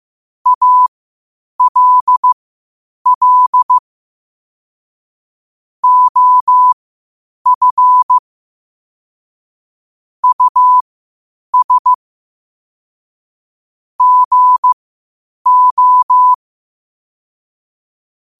Morse code, heard as a keyed tone, about 15 wpm, spaced as for 8 wpm, 1000 Hz. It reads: ALL OF US GO
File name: qotdfiles / 8wpm